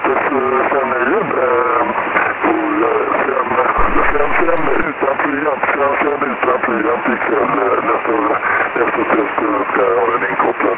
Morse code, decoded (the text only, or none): none